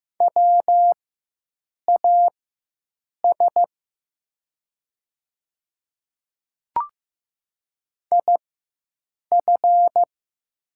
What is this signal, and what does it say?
Morse code, heard as a keyed tone, about 15 wpm, spaced as for 7 wpm, 700 Hz. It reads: WAS IF
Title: 4x Character Spacing - Top 100 Words - 15wpm